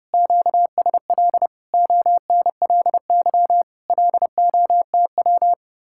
Morse code, heard as a keyed tone, about 35 wpm, 700 Hz. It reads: QSL ONLY LOTW
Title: Speed-Racing Common QSO Phrases - 30wpm